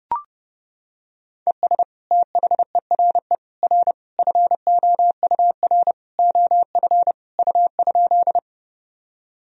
Morse code, heard as a keyed tone, about 32 wpm, 700 Hz. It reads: ES THERE R FOUR OF U?
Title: Sentences from Top 300 Words and Abbreviations 30wpm